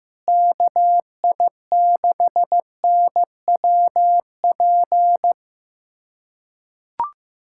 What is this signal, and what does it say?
Morse code, heard as a keyed tone, about 15 wpm, 700 Hz. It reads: KI6NWP
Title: USA Callsign DE Callsign - 6 Characters - 15wpm